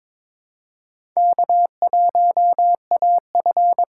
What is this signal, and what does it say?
Morse code, heard as a keyed tone, about 22 wpm, 700 Hz. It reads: K1AF
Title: USA Call Signs - 5 Characters - 6x Character Spacing - 22wpm